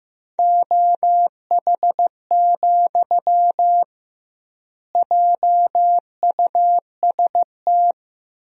Morse code, heard as a keyed tone, about 15 wpm, 700 Hz. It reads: OH, JUST
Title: 2x Word-Spacing - Sentences from Top 500 Words - 15wpm